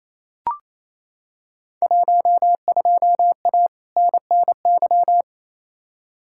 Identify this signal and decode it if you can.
Morse code, heard as a keyed tone, about 28 wpm, 700 Hz. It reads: 12A NNY
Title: Field Day Exchanges Encore - 28wpm